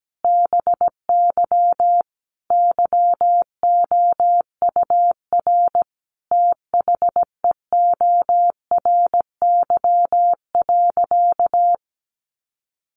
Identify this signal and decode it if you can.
Morse code, heard as a keyed tone, about 17 wpm, 700 Hz. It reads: BY YOUR THEORY.
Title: Sentences from Top 800 Words 17wpm